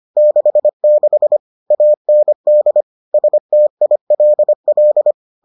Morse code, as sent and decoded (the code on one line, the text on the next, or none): -.... -.... / .- -. -.. / ... - .. .-.. .-..
66 AND STILL